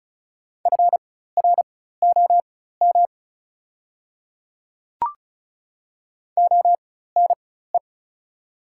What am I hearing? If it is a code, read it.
Morse code, heard as a keyed tone, about 35 wpm, spaced as for 16 wpm, 700 Hz. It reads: FROM ONE
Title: 4x Character Spacing - Top 300 Words - 35wpm